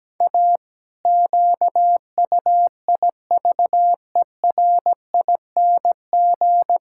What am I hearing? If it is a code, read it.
Morse code, heard as a keyed tone, about 17 wpm, 700 Hz. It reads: A QUIVERING
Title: Pangram Challenge v2 - 17wpm